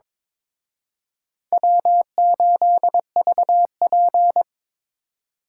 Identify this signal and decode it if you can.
Morse code, heard as a keyed tone, about 26 wpm, 700 Hz. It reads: W8VP
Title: Speed Racing - USA 1x2 Call Signs 22wpm